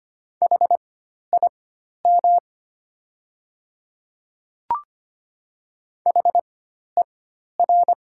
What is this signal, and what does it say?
Morse code, heard as a keyed tone, about 25 wpm, spaced as for 12 wpm, 700 Hz. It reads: HIM HER
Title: 4x Character Spacing - Top 100 Words - 25wpm